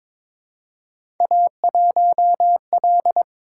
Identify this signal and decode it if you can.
Morse code, heard as a keyed tone, about 22 wpm, 700 Hz. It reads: A1L
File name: practice / 22wpm / Three Letter-Number Combinations - 22wpm